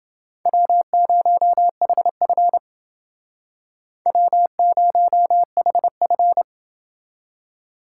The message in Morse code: W0HF W0HF

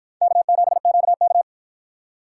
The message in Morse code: K7QK

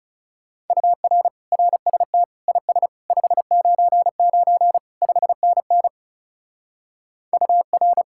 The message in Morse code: UR RST IS 599 5NN UR

